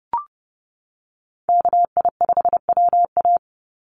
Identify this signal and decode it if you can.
Morse code, heard as a keyed tone, about 30 wpm, 700 Hz. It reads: KI5WA
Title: NA SOTA Chaser Callsigns 30wpm